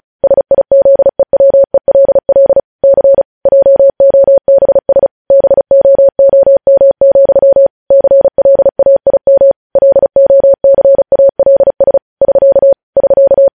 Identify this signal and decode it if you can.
Morse code, heard as a keyed tone, about 36 wpm, 550 Hz. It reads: SIZEWELL C JOBS BOOM, CLAIM LOCALS <SK> <SK>
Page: News headlines in Morse code at 35 wpm